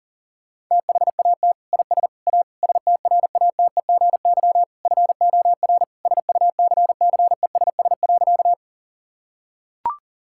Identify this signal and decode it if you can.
Morse code, heard as a keyed tone, about 40 wpm, 700 Hz. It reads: THAT IS A STRATEGY FOR SUCCESS.
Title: Sentences from Top 1000 Words - Review Part 5 - 40wpm